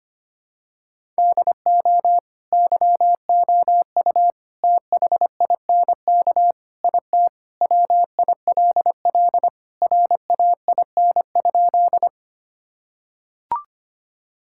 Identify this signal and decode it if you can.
Morse code, heard as a keyed tone, about 25 wpm, 700 Hz. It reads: DO YOU THINK IT WILL RAIN?
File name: practice / 25wpm / Sentences with 4-Word Phrases - 25wpm